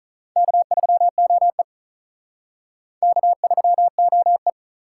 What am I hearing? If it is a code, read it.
Morse code, heard as a keyed tone, about 44 wpm, 700 Hz. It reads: K3OE K3OE